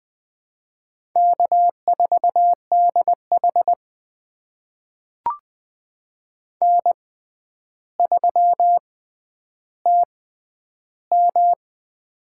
Morse code, heard as a keyed tone, about 20 wpm, spaced as for 10 wpm, 700 Hz. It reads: K4DH N3TM
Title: USA Call Signs - 4 Characters - 6x Character Spacing - 20wpm